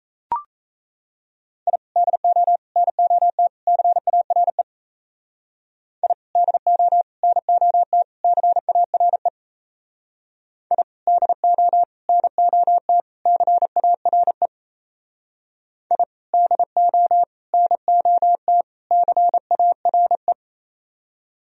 Morse code, heard as a keyed tone, about 36 wpm, 700 Hz. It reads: I DO NOT CARE I DO NOT CARE I DO NOT CARE I DO NOT CARE